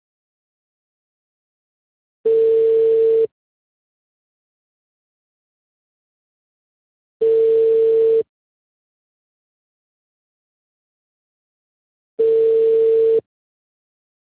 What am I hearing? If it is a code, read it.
Ringback tone (China, 450 Hz).